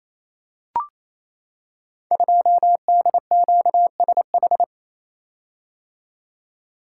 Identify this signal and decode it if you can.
Morse code, heard as a keyed tone, about 27 wpm, 700 Hz. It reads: 2DQSH
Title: Code Groups - 5 Letter:Numbers - 28wpm